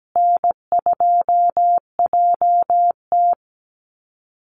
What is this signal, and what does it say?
Morse code, heard as a keyed tone, about 17 wpm, 700 Hz. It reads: N2JT
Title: Easy Call Signs Repeated 3x - 17wpm